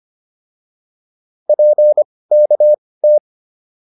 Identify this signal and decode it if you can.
Morse code, heard as a keyed tone, about 25 wpm, spaced as for 18 wpm, 600 Hz. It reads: PKT